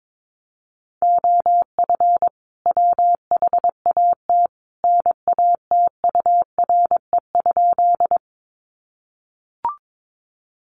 Morse code, heard as a keyed tone, about 22 wpm, 700 Hz. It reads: OF WHAT NATURE?
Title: Sentences from Top 600 Words 22wpm